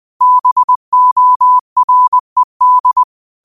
Morse code, heard as a keyed tone, about 20 wpm, 1000 Hz. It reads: BORED
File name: qotdfiles / 20wpm